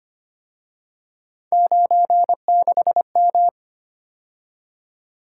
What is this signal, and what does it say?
Morse code, heard as a keyed tone, about 30 wpm, 700 Hz. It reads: 96M
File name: Speed-Racing - Three Letter-Number Combinations - 25wpm